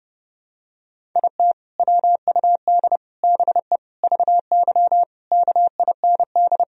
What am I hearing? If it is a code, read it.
Morse code, heard as a keyed tone, about 30 wpm, 700 Hz. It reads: IT WUD BE VY KIND
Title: Sentences from Top 200 Words and Abbreviations 30wpm